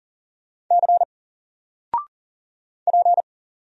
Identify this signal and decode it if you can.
Morse code, heard as a keyed tone, about 39 wpm, 700 Hz. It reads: C P
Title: Single Character (Letter:Number:Punctuation:Prosign) 40wpm